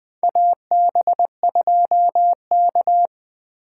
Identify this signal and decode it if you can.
Morse code, heard as a keyed tone, about 20 wpm, 700 Hz. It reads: AB2K
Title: USA Call Signs - 4 Characters - 8x Character Spacing - 20wpm